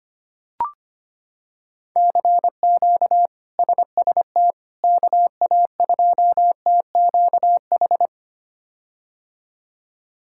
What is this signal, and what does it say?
Morse code, heard as a keyed tone, about 25 wpm, 700 Hz. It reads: CQ SST KA2TQH
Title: K1USN SST QSO Practice - 25wpm